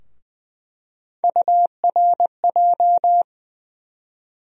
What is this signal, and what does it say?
Morse code, heard as a keyed tone, about 20 wpm, 700 Hz. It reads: URJ
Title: USA Call Sign Postfixes - 3 Letters 20wpm